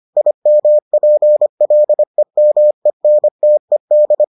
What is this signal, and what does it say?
Morse code, heard as a keyed tone, about 25 wpm, 600 Hz. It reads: IMPLEMENTED